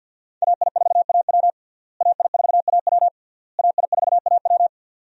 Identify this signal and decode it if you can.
Morse code, heard as a keyed tone, about 50 wpm, 700 Hz. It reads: AI4AW AI4AW AI4AW
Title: USA Call Signs - 5 Characters - Repeated 3x - 50wpm